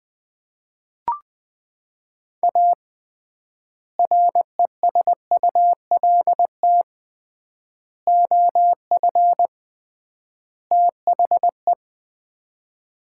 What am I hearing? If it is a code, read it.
Morse code, heard as a keyed tone, about 21 wpm, 700 Hz. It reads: A RESULT OF THE
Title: 3x Word-Spacing - Sets of 4 Words - 20wpm